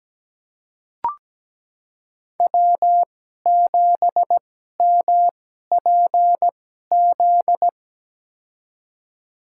Morse code, heard as a keyed tone, about 17 wpm, spaced as for 12 wpm, 700 Hz. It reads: W7MPZ
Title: USA Call Signs - 5 Characters - 2x Character Spacing - 17wpm